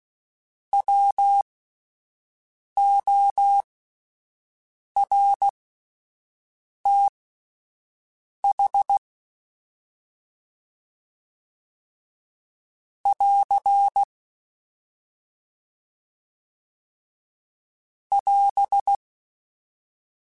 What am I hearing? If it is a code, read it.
Morse code, heard as a keyed tone, about 16 wpm, spaced as for 5 wpm, 800 Hz. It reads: WORTH <AR> <AS>
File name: Alphabetical / Alphabetical mp3s / Ws @5